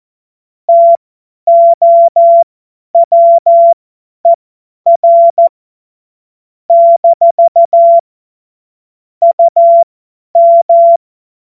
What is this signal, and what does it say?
Morse code, heard as a keyed tone, about 14 wpm, spaced as for 10 wpm, 680 Hz. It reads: TOWER - UM
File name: Morsekurs / Witze / Witz 25. Juli 2023 50 F70